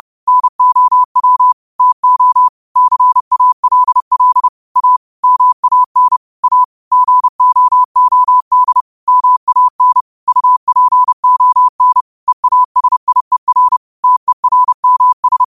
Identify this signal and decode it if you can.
Morse code, heard as a keyed tone, about 30 wpm, 1000 Hz. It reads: NOW TO CALL A MAN A GOOD MAN UPON EASIER TERMS